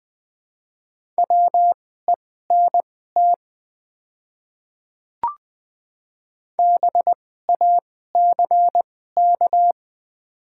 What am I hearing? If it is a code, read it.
Morse code, heard as a keyed tone, about 20 wpm, spaced as for 15 wpm, 690 Hz. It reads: WENT BACK
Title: 2x Character Spacing - Top 200 Words - 20wpm